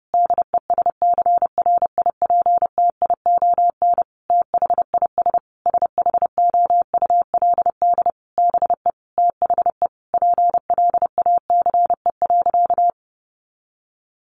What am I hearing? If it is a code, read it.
Morse code, heard as a keyed tone, about 30 wpm, 700 Hz. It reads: DESCRIPTION THIS SHOULD BE THE PLACE.
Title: Sentences from Top 1300 Words 30wpm